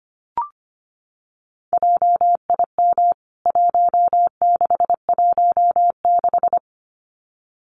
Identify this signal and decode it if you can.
Morse code, heard as a keyed tone, about 25 wpm, 700 Hz. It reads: JIM 1616
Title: CWT Exchange 25wpm